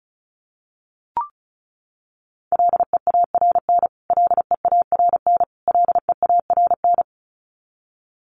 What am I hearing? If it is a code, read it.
Morse code, heard as a keyed tone, about 34 wpm, 700 Hz. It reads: LEARN LEARN LEARN
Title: Top 500 Words Repeated 35wpm